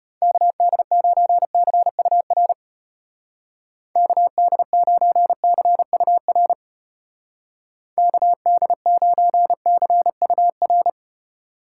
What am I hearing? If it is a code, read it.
Morse code, heard as a keyed tone, about 32 wpm, 700 Hz. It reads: KD9CUR KD9CUR KD9CUR